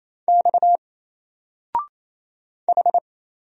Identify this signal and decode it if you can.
Morse code, heard as a keyed tone, about 27 wpm, 700 Hz. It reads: X H
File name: Single-Letter-Number-28wpm